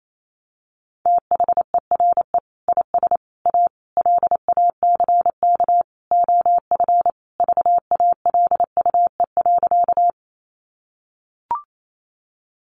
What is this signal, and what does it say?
Morse code, heard as a keyed tone, about 28 wpm, 700 Hz. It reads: THERE IS A LACK OF VALUE.
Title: Sentences from Top 1100 Words 28wpm